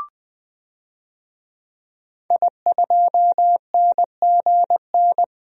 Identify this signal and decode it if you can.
Morse code, heard as a keyed tone, about 20 wpm, 700 Hz. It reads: I2NGN
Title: DXCC-100 Call Signs - 5 Characters - Repeated 2x - 20wpm